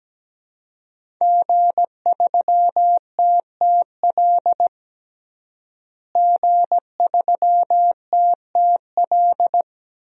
Morse code, heard as a keyed tone, about 17 wpm, 700 Hz. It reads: G3TTL G3TTL
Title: DXCC-100 Call Signs - 5 Characters - Repeated 3x - 17wpm